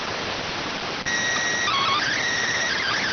SSTV (slow-scan television), PD 290.